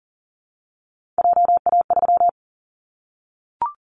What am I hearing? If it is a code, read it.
Morse code, heard as a keyed tone, about 40 wpm, 700 Hz. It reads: JA3
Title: Callsign Prefixes for Top 50 DXCC - 40wpm